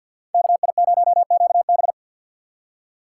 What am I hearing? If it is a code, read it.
Morse code, heard as a keyed tone, about 50 wpm, 700 Hz. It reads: KI0QB
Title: USA Call Signs - 5 Characters - 2x Character Spacing - 50wpm